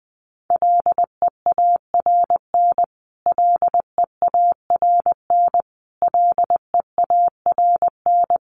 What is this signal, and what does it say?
Morse code, heard as a keyed tone, about 20 wpm, 700 Hz. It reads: LEARN LEARN LEARN
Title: Top 300 Words Repeated 20wpm